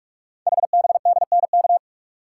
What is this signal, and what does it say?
Morse code, heard as a keyed tone, about 44 wpm, 690 Hz. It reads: HBDNK
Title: Code Groups - 5 Letters - 45wpm